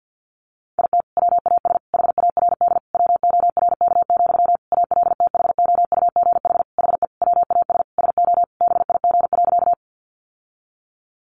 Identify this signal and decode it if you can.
Morse code, heard as a keyed tone, about 48 wpm, 700 Hz. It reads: IT WAS HARD WORK, ALTHOUGH HE WAS SO BIG.
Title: Sentences from Top 600 Words 50wpm